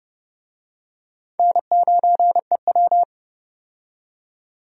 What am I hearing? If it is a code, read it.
Morse code, heard as a keyed tone, about 31 wpm, 700 Hz. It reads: N9EW
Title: USA Call Signs - 4 Characters - 30wpm